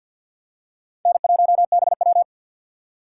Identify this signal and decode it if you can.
Morse code, heard as a keyed tone, about 50 wpm, 700 Hz. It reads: N1BW
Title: USA Call Signs - 2x Character Spacing - 50wpm